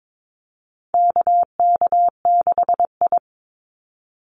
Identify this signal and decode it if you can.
Morse code, heard as a keyed tone, about 22 wpm, 700 Hz. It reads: KK6I
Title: Call Signs Repeated 2x - 22wpm